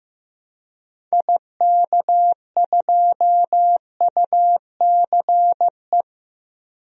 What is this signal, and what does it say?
Morse code, heard as a keyed tone, about 15 wpm, 700 Hz. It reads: IK2UCE
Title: DXCC-100 Call Signs - 6 Characters - Repeated 3x - 15wpm